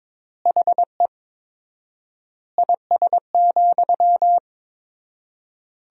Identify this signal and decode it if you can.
Morse code, heard as a keyed tone, about 22 wpm, 700 Hz. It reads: HE IS,
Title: 4x Word-Spacing - Sentences from Top 100 Words - 22wpm